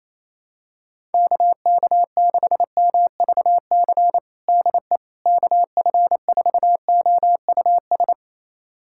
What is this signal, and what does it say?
Morse code, heard as a keyed tone, about 27 wpm, 700 Hz. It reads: KK6MVC DE KF4OUS
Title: USA Callsign DE Callsign - 6 Characters - 28wpm